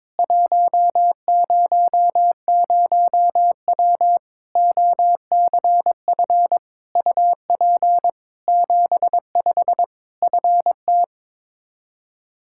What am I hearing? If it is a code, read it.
Morse code, heard as a keyed tone, about 22 wpm, 700 Hz. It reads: 100W OCF UP 75 FT
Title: K1USN SST QSO Practice 22wpm